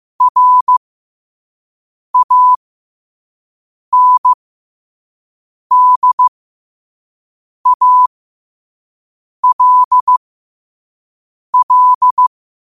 Morse code, heard as a keyed tone, about 15 wpm, spaced as for 5 wpm, 1000 Hz. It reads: RANDALL
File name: qotdfiles / 5wpm